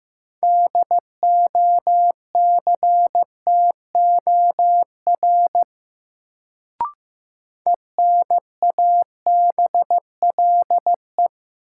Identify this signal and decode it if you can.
Morse code, heard as a keyed tone, about 15 wpm, 700 Hz. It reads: DOCTOR ENABLE
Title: Sentences from Top 1500 Words - New Word Drill - 15wpm